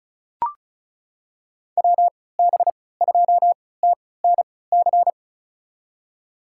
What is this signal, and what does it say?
Morse code, heard as a keyed tone, about 36 wpm, spaced as for 20 wpm, 690 Hz. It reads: WB2TNC
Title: USA Call Signs - 6 Characters - 3x Character Spacing - 35wpm